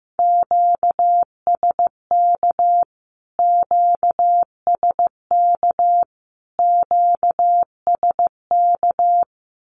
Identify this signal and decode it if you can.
Morse code, heard as a keyed tone, about 15 wpm, 700 Hz. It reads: QSK QSK QSK
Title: Q-Codes Repeated 3x 15wpm